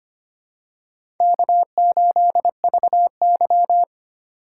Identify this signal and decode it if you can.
Morse code, heard as a keyed tone, about 25 wpm, 700 Hz. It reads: K8VY
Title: USA Call Signs - 3x Character Spacing - 25wpm